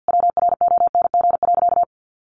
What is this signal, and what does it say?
Morse code, heard as a keyed tone, about 50 wpm, 700 Hz. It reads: WRONG.
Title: Sentences from Top 1300 Words 50wpm